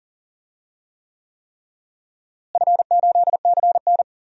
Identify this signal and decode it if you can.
Morse code, heard as a keyed tone, about 40 wpm, 700 Hz. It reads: F8CN
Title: DXCC-100 Call Signs - 4 Characters - Repeated 3x - 40wpm